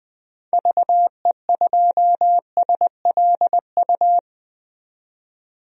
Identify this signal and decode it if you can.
Morse code, heard as a keyed tone, about 20 wpm, 700 Hz. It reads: VE2SLU
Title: DXCC-100 Call Signs - 6 Characters - 20wpm